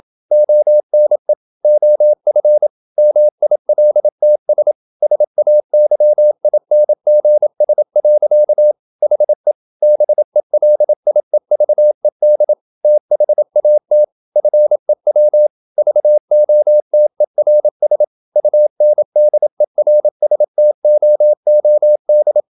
Morse code, heard as a keyed tone, about 27 wpm, 600 Hz. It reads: ONE OF MILTS SAYINGS. HE BELIEVED THAT FEW VOTERS UNDERSTOOD
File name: Never 0027wpm